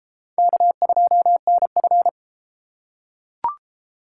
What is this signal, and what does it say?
Morse code, heard as a keyed tone, about 30 wpm, 700 Hz. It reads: K2NF